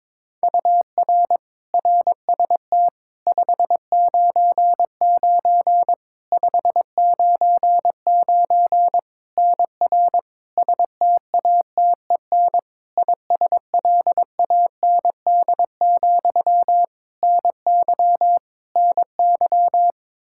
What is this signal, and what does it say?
Morse code, heard as a keyed tone, about 22 wpm, 700 Hz. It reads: UR RST 599 599 NR STATEN ISLAND, NY NY
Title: Practice QSO - Exchange 1 - Beginning - 22wpm